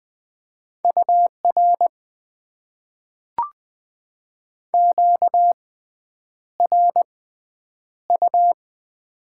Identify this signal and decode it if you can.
Morse code, heard as a keyed tone, about 20 wpm, spaced as for 9 wpm, 700 Hz. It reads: UR QRU
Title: Top 90 QSO Elements - 6x Character Spacing - 20wpm